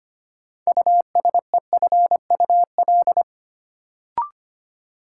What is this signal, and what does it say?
Morse code, heard as a keyed tone, about 25 wpm, 700 Hz. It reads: USEFUL
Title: Sentences from Top 700 Words - New Word Drill - 25wpm